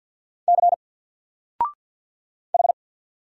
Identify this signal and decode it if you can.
Morse code, heard as a keyed tone, about 48 wpm, 700 Hz. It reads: C H
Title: Single Letter-Number Alt 50wpm